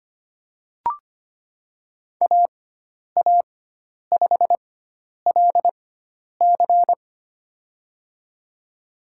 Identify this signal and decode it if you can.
Morse code, heard as a keyed tone, about 25 wpm, spaced as for 10 wpm, 700 Hz. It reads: AA5LC
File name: Lesson 16g - Letter c - Call Signs 25wpm